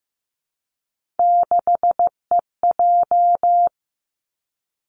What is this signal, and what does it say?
Morse code, heard as a keyed tone, about 15 wpm, 700 Hz. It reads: 6EJ
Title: Three Challenging Letter-Number Combinations 15wpm